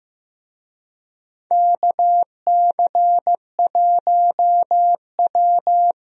Morse code, heard as a keyed tone, about 15 wpm, 700 Hz. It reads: KC1W
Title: USA Call Signs - 4 Characters - 15wpm